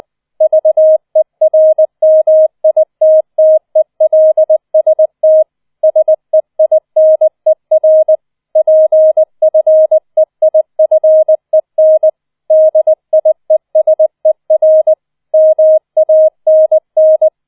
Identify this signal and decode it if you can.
Morse code, heard as a keyed tone, about 20 wpm, 630 Hz. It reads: VERMITTELST SEINER PFEIFEN DIESER MANN